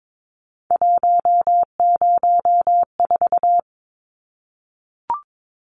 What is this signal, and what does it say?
Morse code, heard as a keyed tone, about 22 wpm, 700 Hz. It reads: 104
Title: Numbers - 3-Digit 22wpm